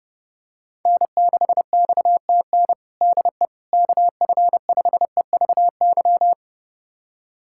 Morse code, heard as a keyed tone, about 30 wpm, 700 Hz. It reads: N6XTN DE KF5EVY
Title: USA Callsign DE Callsign 30wpm